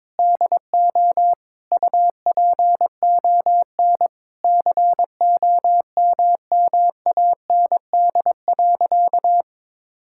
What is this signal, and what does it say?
Morse code, heard as a keyed tone, about 22 wpm, 700 Hz. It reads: DO UPON COMMAND.
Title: Sentences from Top 1600 Words 22wpm